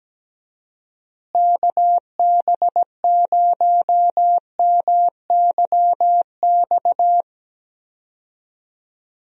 Morse code, heard as a keyed tone, about 17 wpm, 700 Hz. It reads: KB0MYX
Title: USA Call Signs - 6 Characters - Repeated 3x - 17wpm